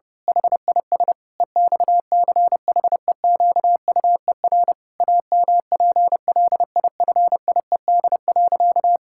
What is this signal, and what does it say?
Morse code, heard as a keyed tone, about 30 wpm, 700 Hz. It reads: HIS EXCHEQUER AMPLIFIED.
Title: Pangram Challenge v2 - 30wpm